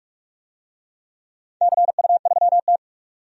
Morse code, heard as a keyed tone, about 44 wpm, 690 Hz. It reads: CU3T
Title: DXCC-100 Call Signs - 4 Characters - 45wpm